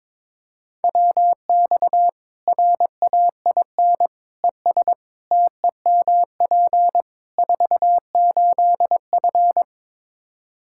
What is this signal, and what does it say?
Morse code, heard as a keyed tone, about 22 wpm, 700 Hz. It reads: WX RAIN ES TEMP 48F
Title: Practice QSO - Exchange 2 - Info - 22wpm